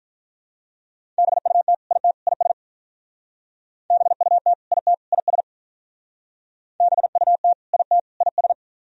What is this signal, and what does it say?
Morse code, heard as a keyed tone, about 47 wpm, 700 Hz. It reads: BUT IT IS BUT IT IS BUT IT IS